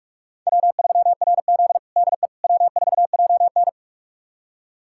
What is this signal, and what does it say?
Morse code, heard as a keyed tone, about 42 wpm, 700 Hz. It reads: W3RZ DE W4JN
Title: USA Callsign DE Callsign - 4 Characters - 45wpm